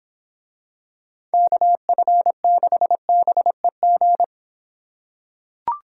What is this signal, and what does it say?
Morse code, heard as a keyed tone, about 26 wpm, 700 Hz. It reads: KF6BEG